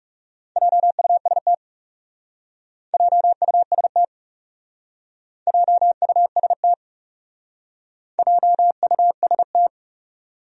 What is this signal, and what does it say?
Morse code, heard as a keyed tone, about 36 wpm, 700 Hz. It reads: JUST JUST JUST JUST
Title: Speed-Racing Top 500 Words 30wpm